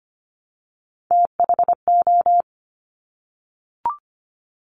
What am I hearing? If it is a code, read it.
Morse code, heard as a keyed tone, about 25 wpm, 700 Hz. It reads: THO
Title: Top Three Letter Combinations in Words 25wpm